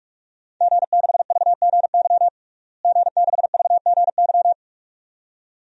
Morse code, heard as a keyed tone, about 46 wpm, 700 Hz. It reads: G6VGY G6VGY
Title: DXCC-100 Call Signs - 5 Characters - Repeated 2x - 45wpm